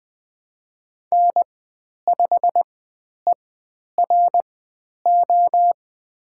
Morse code, heard as a keyed tone, about 20 wpm, spaced as for 10 wpm, 700 Hz. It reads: N5ERO
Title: 102 Lesson 17g - Letter u - Call Signs 20wpm